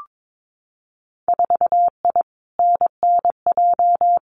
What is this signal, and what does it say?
Morse code, heard as a keyed tone, about 22 wpm, 700 Hz. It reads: 4I NNJ
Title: Winter Field Day Exchanges 22wpm